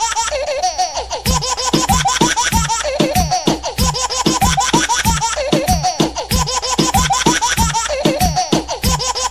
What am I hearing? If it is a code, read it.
There is no code.